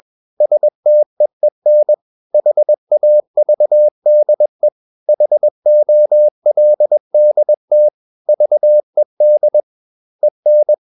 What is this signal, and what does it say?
Morse code, heard as a keyed tone, about 21 wpm, 600 Hz. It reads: STEEN HAVDE HOLDT VED EN
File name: Ferie QRS 21wpm